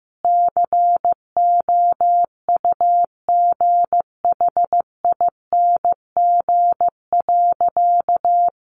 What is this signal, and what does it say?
Morse code, heard as a keyed tone, about 15 wpm, 700 Hz. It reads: COUGHING.
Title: Additional Easy Phrases 15wpm